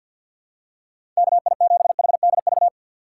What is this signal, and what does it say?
Morse code, heard as a keyed tone, about 50 wpm, 700 Hz. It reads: KI7HDV
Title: USA Call Signs - 6x Character Spacing - 50wpm